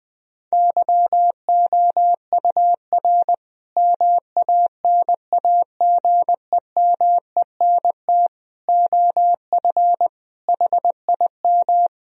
Morse code, heard as a keyed tone, about 20 wpm, 700 Hz. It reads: YOUR MANAGEMENT OF HIM
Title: Sentences from Top 1000 Words - Review Part 4 - 20wpm